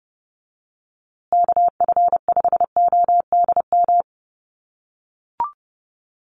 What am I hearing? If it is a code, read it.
Morse code, heard as a keyed tone, about 29 wpm, 700 Hz. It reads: KF5ODM
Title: USA Call Signs Repeated 3x 30wpm